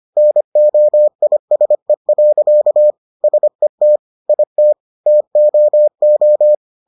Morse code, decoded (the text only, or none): NOISE. SET IT TOO